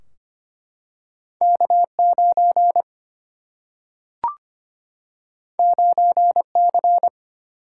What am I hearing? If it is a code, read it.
Morse code, heard as a keyed tone, about 25 wpm, 700 Hz. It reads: K9 9C
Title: Single-Letter Single-Number 25wpm